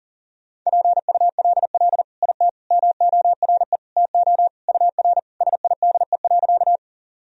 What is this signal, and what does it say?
Morse code, heard as a keyed tone, about 40 wpm, 700 Hz. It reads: PULL IT MORE TO UR SIDE.
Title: Sentences from Top 400 Words and Abbreviations 40wpm